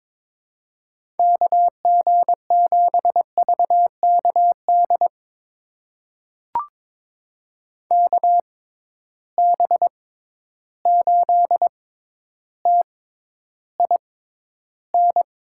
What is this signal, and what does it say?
Morse code, heard as a keyed tone, about 23 wpm, spaced as for 11 wpm, 700 Hz. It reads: KG7VKD KB8TIN
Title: USA Call Signs - 6 Characters - 6x Character Spacing - 22wpm